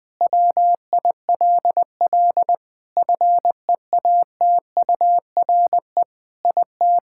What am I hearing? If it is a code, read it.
Morse code, heard as a keyed tone, about 20 wpm, 700 Hz. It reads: WILL FEATURE IT
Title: Sentences from Top 1000 Words - Review Part 5 - 20wpm